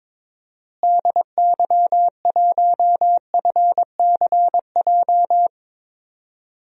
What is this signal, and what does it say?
Morse code, heard as a keyed tone, about 21 wpm, 700 Hz. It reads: DY1FCJ
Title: DXCC-100 Call Signs - 6 Characters - Repeated 3x - 22wpm